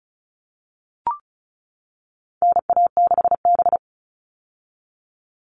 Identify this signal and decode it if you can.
Morse code, heard as a keyed tone, about 34 wpm, 700 Hz. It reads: NA6B
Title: All Cut Numbers - 4-Digit 35wpm